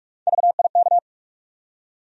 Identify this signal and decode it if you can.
Morse code, heard as a keyed tone, about 42 wpm, 700 Hz. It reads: VIK
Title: USA Call Sign Postfixes - 3 Letters 45wpm